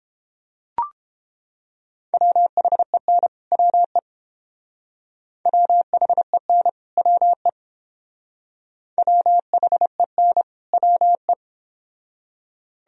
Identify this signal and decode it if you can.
Morse code, heard as a keyed tone, about 30 wpm, 700 Hz. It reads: WHEN WE WHEN WE WHEN WE